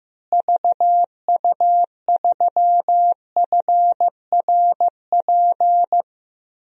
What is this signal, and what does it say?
Morse code, heard as a keyed tone, about 15 wpm, 700 Hz. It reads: VU3FRP
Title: DXCC-100 Call Signs - 6 Characters - Repeated 3x - 15wpm